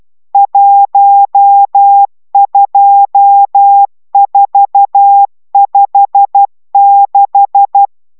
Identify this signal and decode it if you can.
Morse code, heard as a keyed tone, about 12 wpm, 800 Hz. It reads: 12456
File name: taviro tanulas / cw 69 22 10